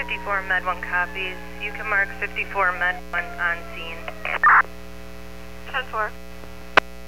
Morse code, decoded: none

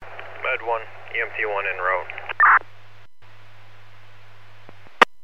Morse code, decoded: none

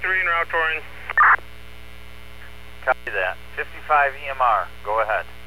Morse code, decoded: none

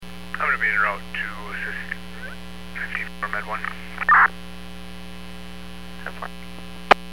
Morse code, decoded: none

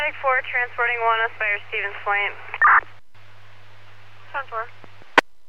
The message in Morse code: none

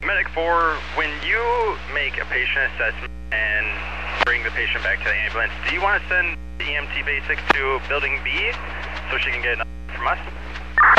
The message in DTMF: none

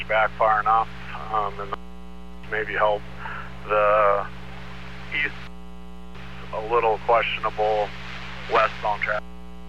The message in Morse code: none

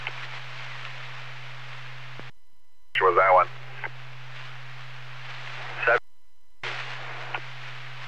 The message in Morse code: none

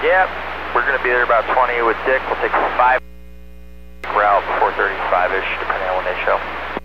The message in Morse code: none

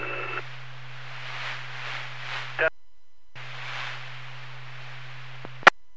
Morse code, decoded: none

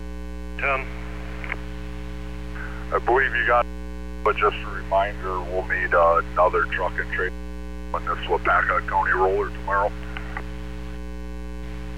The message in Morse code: none

